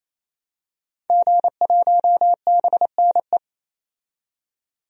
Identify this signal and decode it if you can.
Morse code, heard as a keyed tone, about 28 wpm, 700 Hz. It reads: G1BNE
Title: DXCC-100 Call Signs - 5 Characters - 28wpm